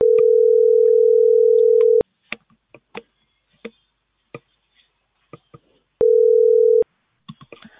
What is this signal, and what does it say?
Ringback tone (North America), 440 + 480 Hz.